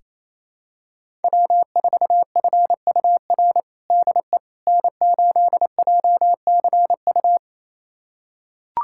Morse code, heard as a keyed tone, about 28 wpm, 700 Hz. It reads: W4FUR DE N8JCU
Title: USA Callsign DE Callsign - 5 Characters - 28wpm